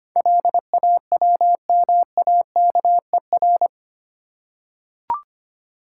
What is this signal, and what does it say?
Morse code, heard as a keyed tone, about 25 wpm, 690 Hz. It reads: LAWMAKER
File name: 8 Letter Words - First 3 Letters Given - 25wpm